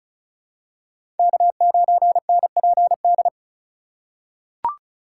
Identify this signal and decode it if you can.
Morse code, heard as a keyed tone, about 36 wpm, 700 Hz. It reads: K9NPD
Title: USA Call Signs - 4x Character Spacing - 35wpm